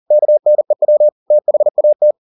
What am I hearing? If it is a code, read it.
Morse code, heard as a keyed tone, about 40 wpm, 600 Hz. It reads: KNEW THAT
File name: the old man and the sea 2 40wpm